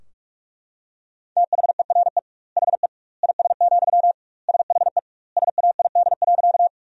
Morse code, heard as a keyed tone, about 42 wpm, 700 Hz. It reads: THERE HE IS, SHE SAID.
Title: 2x Word-Spacing - Sentences from Top 100 Words - 45wpm